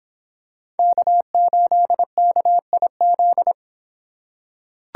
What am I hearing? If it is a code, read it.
Morse code, heard as a keyed tone, about 26 wpm, 700 Hz. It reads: K8KIZ